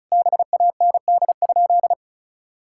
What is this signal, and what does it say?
Morse code, heard as a keyed tone, about 34 wpm, 700 Hz. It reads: BAND?
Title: Sentences from Top 1600 Words 35wpm